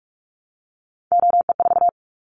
Morse code, heard as a keyed tone, about 44 wpm, 700 Hz. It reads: OE4
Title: Callsign Prefixes for Top 100 DXCC - 45wpm